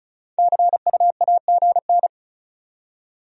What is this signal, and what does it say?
Morse code, heard as a keyed tone, about 46 wpm, 700 Hz. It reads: CUAGN